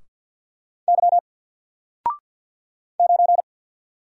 Morse code, heard as a keyed tone, about 50 wpm, 700 Hz. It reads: Y 9
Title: Single Letter-Number 50wpm